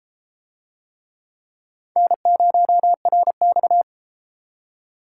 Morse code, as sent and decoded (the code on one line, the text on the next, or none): -. ----- .-. -..-
N0RX